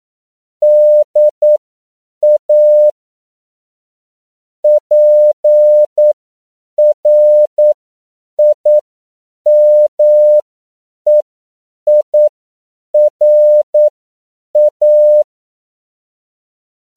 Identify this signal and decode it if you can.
Morse code, heard as a keyed tone, about 9 wpm, spaced as for 7 wpm, 600 Hz. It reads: DA PRIMEIRA